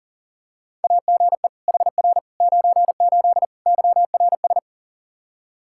Morse code, heard as a keyed tone, about 40 wpm, 700 Hz. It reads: AGE HR 98 YRS
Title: K1USN SST QSO Practice 40wpm